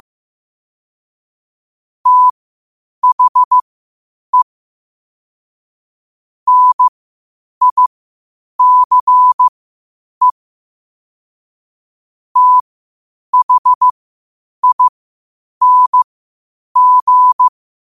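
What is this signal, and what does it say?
Morse code, heard as a keyed tone, about 15 wpm, spaced as for 8 wpm, 1000 Hz. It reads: THE NICE THING
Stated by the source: Morse code quotes at 8 words per minute from QOTD at 8 WPM.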